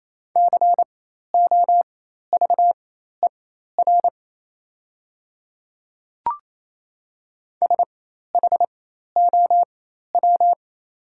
Morse code, heard as a keyed tone, about 28 wpm, spaced as for 13 wpm, 690 Hz. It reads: COVER SHOW
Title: 4x Character Spacing - Top 500 Words - 28wpm